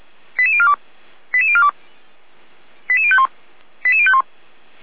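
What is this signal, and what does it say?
Five-tone selective call (ZVEI), heard as quick runs of five tones, tones of about 70 ms. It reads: 88042 88042 88051 88051
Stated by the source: ZVEI five-tone call received by SDR.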